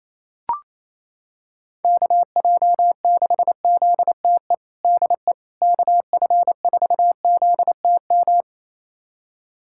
Morse code, heard as a keyed tone, about 28 wpm, 700 Hz. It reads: KJ6ZTE DE KF4ZTM
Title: USA Callsign DE Callsign - 6 Characters - 28wpm